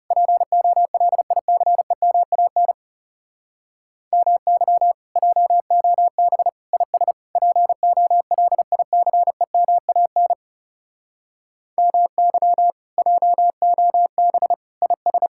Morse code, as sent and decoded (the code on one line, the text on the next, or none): .--. --- .-.. .. -.-. -- .- -. / -- -.-- / .--- --- -... / .. ... / .--. --- .-.. .. -.-. -- .- -. / -- -.-- / .--- --- -... / .. ...
POLICEMAN MY JOB IS POLICEMAN MY JOB IS